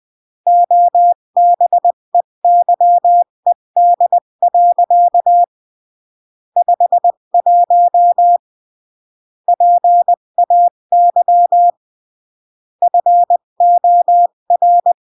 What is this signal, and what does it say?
Morse code, heard as a keyed tone, about 20 wpm, 700 Hz. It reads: OBEYED. 51 PAY FOR
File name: CW / jodorowsky / 82 commandments 20-15